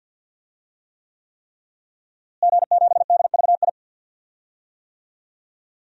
Morse code, heard as a keyed tone, about 52 wpm, 700 Hz. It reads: G7DVI
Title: DXCC-100 Call Signs - 5 Characters - Repeated 3x - 50wpm